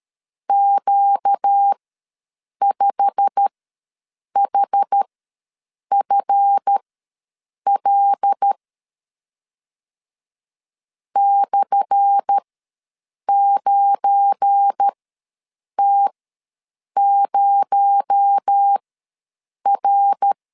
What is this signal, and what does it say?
Morse code, heard as a keyed tone, about 12 wpm, spaced as for 7 wpm, 800 Hz. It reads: Q5HFL /9T0R
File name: LEZIONI / 10 SIGNS / 10A SLASH ws 12wpm 3443